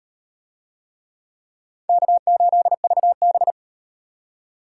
Morse code, heard as a keyed tone, about 35 wpm, 700 Hz. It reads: K8VB